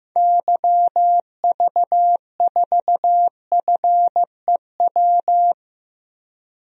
Morse code, heard as a keyed tone, about 15 wpm, 700 Hz. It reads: YV4FEW
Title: DXCC-100 Call Signs - 6 Characters - 15wpm